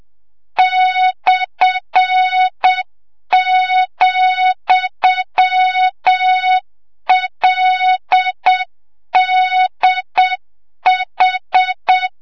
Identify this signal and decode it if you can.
Morse code, heard as a keyed tone, about 7 wpm, 3700 Hz, with noise. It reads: /,LDH
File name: Morze-HA2MN / hang / 25 GYAKORLAT